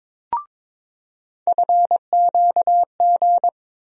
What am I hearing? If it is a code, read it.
Morse code, heard as a keyed tone, about 21 wpm, 700 Hz. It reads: FQG
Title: USA Call Sign Postfixes - 3 Letters 22wpm